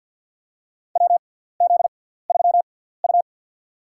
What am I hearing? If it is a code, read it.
Morse code, heard as a keyed tone, about 50 wpm, spaced as for 17 wpm, 700 Hz. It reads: WZ3U
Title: USA Call Signs - 4 Characters - 6x Character Spacing - 50wpm